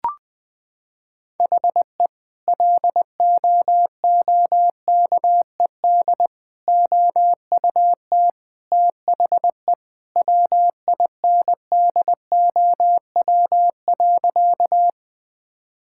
Morse code, heard as a keyed tone, about 20 wpm, 700 Hz. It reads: HE LOOKED OUT THE WINDOW.
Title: Sentences with 2-Word Phrases - 20wpm